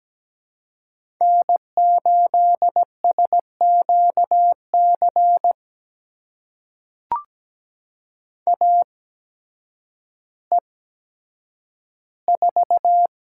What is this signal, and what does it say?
Morse code, heard as a keyed tone, about 17 wpm, spaced as for 9 wpm, 700 Hz. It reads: N8SQC AE4
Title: USA Call Signs - 5 Characters - 8x Character Spacing - 17wpm